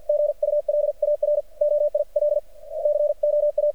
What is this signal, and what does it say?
Morse code, heard as a keyed tone, about 48 wpm, 590 Hz. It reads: GULAR OIL COA
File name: archived / CRSnet